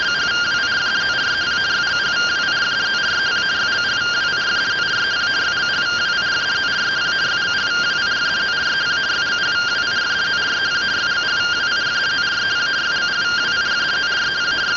Two-tone teleprinter signal, about 170 Hz shift.